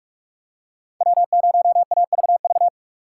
Morse code, heard as a keyed tone, about 44 wpm, 700 Hz. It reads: W0AVV